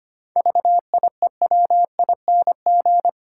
Morse code, heard as a keyed tone, about 25 wpm, 700 Hz. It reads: VIEWING
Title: Pangram Challenge v2 - 25wpm